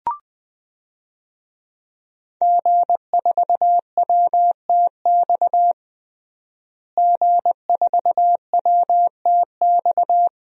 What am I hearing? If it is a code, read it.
Morse code, heard as a keyed tone, about 20 wpm, 700 Hz. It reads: G4WTX G4WTX
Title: DXCC-100 Call Signs - 5 Characters - Repeated 3x - 20wpm